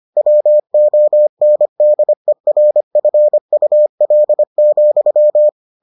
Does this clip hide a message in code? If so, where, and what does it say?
Yes: Morse code from 0.2 to 5.5 s, heard as a keyed tone, about 25 wpm, 600 Hz. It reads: WONDERFUL,